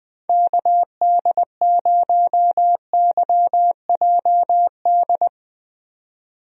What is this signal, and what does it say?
Morse code, heard as a keyed tone, about 20 wpm, 700 Hz. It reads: KD0YJD
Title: USA Call Signs - 6 Characters - 20wpm